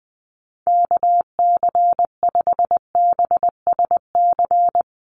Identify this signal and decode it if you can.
Morse code, heard as a keyed tone, about 20 wpm, 700 Hz. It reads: KC5BSC
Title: USA Call Signs Repeated 3x 20wpm